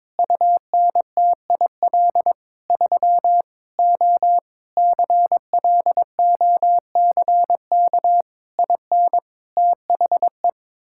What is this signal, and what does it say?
Morse code, heard as a keyed tone, about 22 wpm, 700 Hz. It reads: UNTIL 3 O CLOCK IN THE
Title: Pangram Challenge v2 - 22wpm